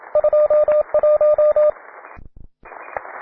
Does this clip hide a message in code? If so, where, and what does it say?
Yes: Morse code from 0.2 to 1.7 s, heard as a keyed tone, about 28 wpm, 600 Hz, with radio noise. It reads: 21